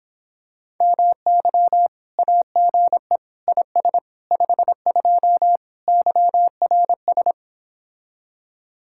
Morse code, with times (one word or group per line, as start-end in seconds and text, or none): 0.8-1.9 MY
2.2-3.2 AGE
3.5-4.0 IS
4.3-5.6 52
5.9-7.3 YRS